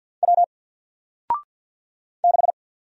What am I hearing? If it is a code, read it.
Morse code, heard as a keyed tone, about 49 wpm, 700 Hz. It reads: W 6
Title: Single Character (Letter:Number:Punctuation:Prosign) Alt 50wpm